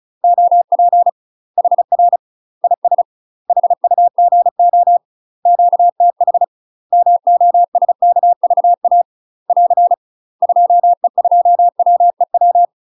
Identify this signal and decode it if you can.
Morse code, heard as a keyed tone, about 34 wpm, 700 Hz. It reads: OP HR IS HUGO QTH MOSKVA <AR> 2E2WEW
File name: UK MASTER QSOs TRAINING / 35WPM / UK MASTER QSOs 0008